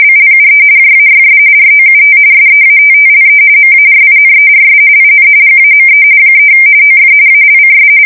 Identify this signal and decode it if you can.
RTTY, about 45.45 baud. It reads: BLUE JAYS ROUT SEATTLE MARINERS IN GAME 3 OF ALCS
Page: 170 Shift 2125/2295